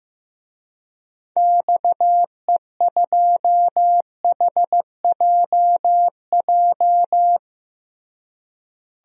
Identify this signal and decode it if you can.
Morse code, heard as a keyed tone, about 15 wpm, 700 Hz. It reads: XE2HJJ
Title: DXCC-100 Call Signs - 6 Characters - Repeated 3x - 15wpm